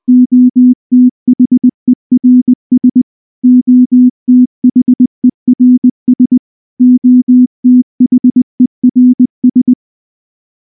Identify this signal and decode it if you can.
Morse code, heard as a keyed tone, about 19 wpm, 250 Hz. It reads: OTHERS OTHERS OTHERS
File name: morse / morse-training-20WPM-4